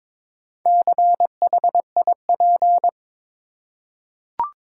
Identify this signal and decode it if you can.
Morse code, heard as a keyed tone, about 21 wpm, 700 Hz. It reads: CHIP
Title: Four Letter Words - 22wpm